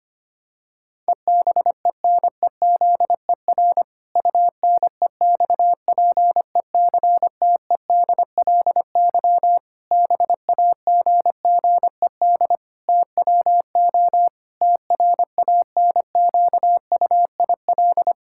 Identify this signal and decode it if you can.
Morse code, heard as a keyed tone, about 25 wpm, 700 Hz. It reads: EBENEZER UNEXPECTEDLY BAGGED TWO TRANQUIL
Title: Pangram Challenge v2 - 25wpm